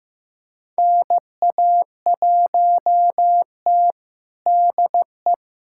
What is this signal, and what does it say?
Morse code, heard as a keyed tone, about 15 wpm, 700 Hz. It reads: NA1T DE
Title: USA Callsign DE Callsign - 4 Characters - 15wpm